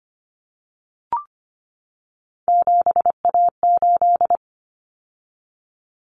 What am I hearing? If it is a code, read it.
Morse code, heard as a keyed tone, about 25 wpm, 700 Hz. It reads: 7A8
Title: Common Cut Numbers - 3-Digit 25wpm